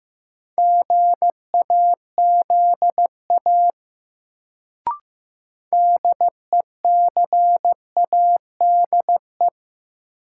Morse code, heard as a keyed tone, about 15 wpm, 700 Hz. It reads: GAZA DECADE
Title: Sets of 5 Words - New Word Drill - 15wpm